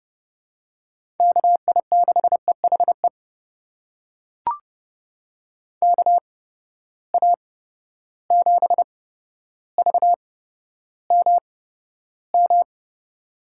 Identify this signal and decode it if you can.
Morse code, heard as a keyed tone, about 31 wpm, spaced as for 13 wpm, 700 Hz. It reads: KI6EHE KA7VMM
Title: USA Call Signs - 6 Characters - 8x Character Spacing - 30wpm